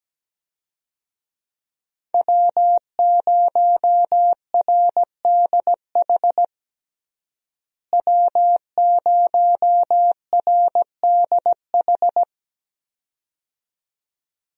Morse code, heard as a keyed tone, about 17 wpm, 700 Hz. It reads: W0RDH W0RDH
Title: USA Call Signs - 5 Characters - Repeated 2x - 17wpm